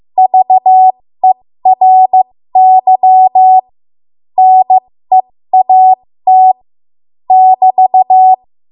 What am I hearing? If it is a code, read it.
Morse code, heard as a keyed tone, about 15 wpm, spaced as for 13 wpm, 750 Hz. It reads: VERY NEAT <BT>
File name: Morse / 13WPM